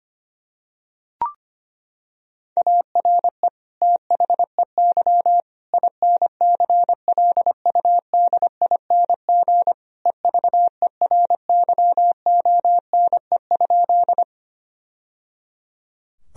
Morse code, heard as a keyed tone, about 25 wpm, 700 Hz. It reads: ARE THEY INCLUDING EVERYONE?
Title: Sentences from Top 1000 Words - Review Part 4 - 25wpm